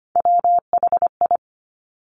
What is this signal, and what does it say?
Morse code, heard as a keyed tone, about 25 wpm, 700 Hz. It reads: WHI
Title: Top Three Letter Combinations in Words 25wpm